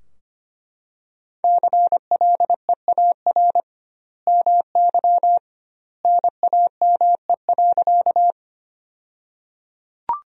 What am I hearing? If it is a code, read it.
Morse code, heard as a keyed tone, about 25 wpm, 700 Hz. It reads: CLEAR MY NAME.
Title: 2x Word-Spacing - Sentences from Top 500 Words - 25wpm